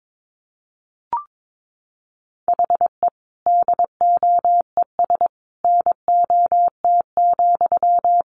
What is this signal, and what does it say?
Morse code, heard as a keyed tone, about 22 wpm, 700 Hz. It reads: HE DOES NOT,
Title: Sentences from Top 1200 Words 22wpm